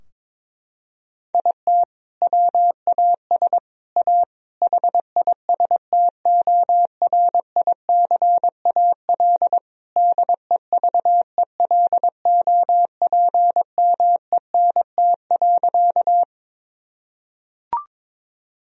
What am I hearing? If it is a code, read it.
Morse code, heard as a keyed tone, about 23 wpm, 700 Hz. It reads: IT WAS A HISTORICAL DEVELOPMENT.
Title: Sentences from Top 1000 Words - Review Part 4 - 22wpm